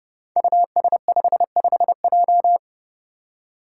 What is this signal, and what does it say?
Morse code, heard as a keyed tone, about 30 wpm, 700 Hz. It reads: US55J
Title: Five Challenging Letter-Number Combinations 30wpm